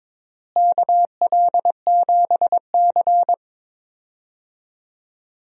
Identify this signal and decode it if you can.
Morse code, heard as a keyed tone, about 22 wpm, 700 Hz. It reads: KL7C
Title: DXCC-100 Call Signs - 4 Characters - Repeated 2x - 22wpm